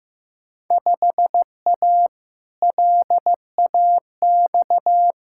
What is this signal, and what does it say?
Morse code, heard as a keyed tone, about 16 wpm, 700 Hz. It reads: 5A LAX
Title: Field Day Exchanges - 15wpm